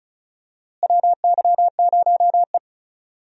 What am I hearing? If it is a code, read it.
Morse code, heard as a keyed tone, about 34 wpm, 700 Hz. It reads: WY0E
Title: USA Call Signs - 4 Characters - Repeated 3x - 35wpm